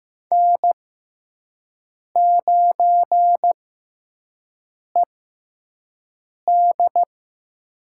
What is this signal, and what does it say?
Morse code, heard as a keyed tone, about 15 wpm, spaced as for 5 wpm, 700 Hz. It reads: N9ED
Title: USA Call Signs - 4 Characters - 6x Character Spacing - 15wpm